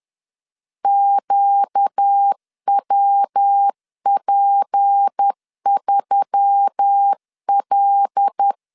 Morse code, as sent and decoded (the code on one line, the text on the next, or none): --.- .-- .--. ...-- .-..
QWP3L